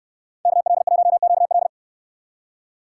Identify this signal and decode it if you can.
Morse code, heard as a keyed tone, about 69 wpm, 700 Hz. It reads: KL1QL